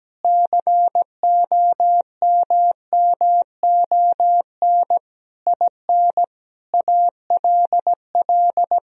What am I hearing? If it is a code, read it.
Morse code, heard as a keyed tone, about 17 wpm, 700 Hz. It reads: COMMON IN ALL
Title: Sentences from Top 1000 Words - Review Part 3 - 17wpm